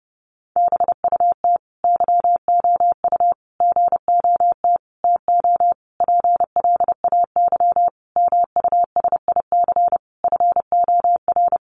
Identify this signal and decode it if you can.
Morse code, heard as a keyed tone, about 30 wpm, 700 Hz. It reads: BUT YOU GOT TO PLAY MUSIC FOR
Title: Sentences from Top 300 Words 30wpm